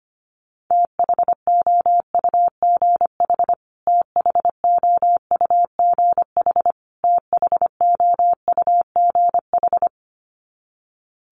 Morse code, heard as a keyed tone, about 25 wpm, 700 Hz. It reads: THOUGH THOUGH THOUGH
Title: Top 500 Words Repeated 25wpm